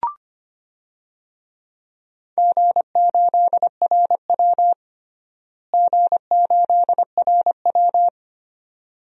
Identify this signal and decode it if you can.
Morse code, heard as a keyed tone, about 25 wpm, 700 Hz. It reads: G8RW G8RW
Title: DXCC-100 Call Signs - 4 Characters - Repeated 2x - 25wpm